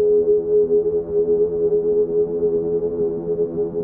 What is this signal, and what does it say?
Congestion tone (425 Hz).